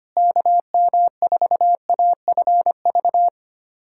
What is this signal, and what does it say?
Morse code, heard as a keyed tone, about 25 wpm, 700 Hz. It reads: KM4AFV
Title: USA Callsign DE Callsign - 6 Characters - 25wpm